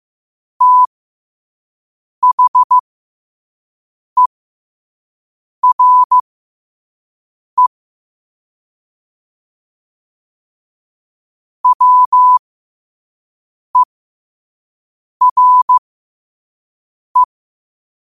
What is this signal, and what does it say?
Morse code, heard as a keyed tone, about 15 wpm, spaced as for 5 wpm, 1000 Hz. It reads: THERE WERE
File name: qotdfiles / 5wpm